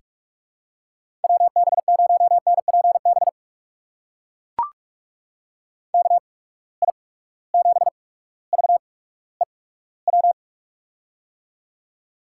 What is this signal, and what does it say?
Morse code, heard as a keyed tone, about 44 wpm, spaced as for 19 wpm, 700 Hz. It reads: WB0NPB KI7VEW
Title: USA Call Signs - 6 Characters - 8x Character Spacing - 45wpm